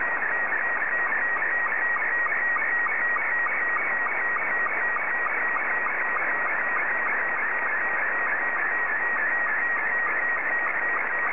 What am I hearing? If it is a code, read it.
SSTV (slow-scan television), Robot 72.
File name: sstv / R72-VK3EME